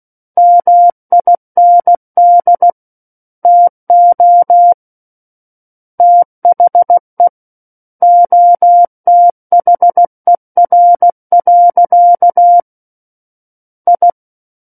Morse code, heard as a keyed tone, about 16 wpm, 700 Hz. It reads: MIND TO THE OTHER. I
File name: Book One / 07-Chapter 7 - 16 WPM